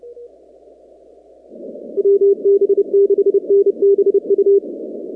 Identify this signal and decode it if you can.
Morse code, heard as a keyed tone, about 30 wpm, 400 Hz. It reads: WB6NBU
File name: contests / 2019-cw-wpx-kt5j-k5pi- -2330 0-19